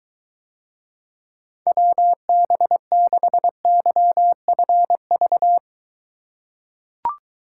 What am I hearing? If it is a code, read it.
Morse code, heard as a keyed tone, about 23 wpm, 700 Hz. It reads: WB6YFV